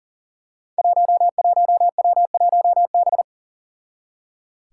Morse code, heard as a keyed tone, about 39 wpm, 700 Hz. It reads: 11W1B
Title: Five Challenging Letter-Number Combinations 40wpm